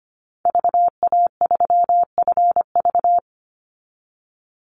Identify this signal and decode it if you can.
Morse code, heard as a keyed tone, about 25 wpm, 700 Hz. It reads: VA3FV
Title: Call Signs Repeated 2x - 25wpm